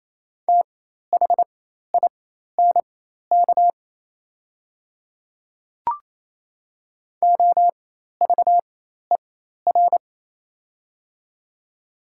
Morse code, heard as a keyed tone, about 28 wpm, spaced as for 13 wpm, 700 Hz. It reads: THINK OVER
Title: 4x Character Spacing - Top 200 Words - 28wpm